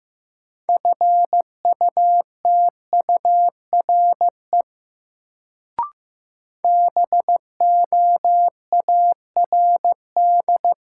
Morse code, heard as a keyed tone, about 15 wpm, 700 Hz. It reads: FUTURE BOARD
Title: Sentences from Top 700 Words - New Word Drill - 15wpm